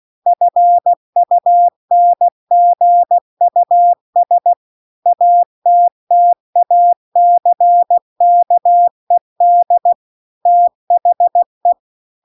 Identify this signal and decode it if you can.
Morse code, heard as a keyed tone, about 16 wpm, 700 Hz. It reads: FUNGUS ATTACKED THE